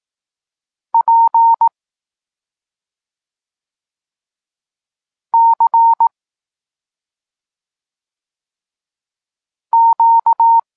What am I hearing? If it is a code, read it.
Morse code, heard as a keyed tone, about 18 wpm, 920 Hz. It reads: P C Q